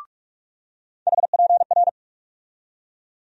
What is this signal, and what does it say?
Morse code, heard as a keyed tone, about 42 wpm, 700 Hz. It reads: HPR
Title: USA Call Sign Postfixes - 3 Letters 45wpm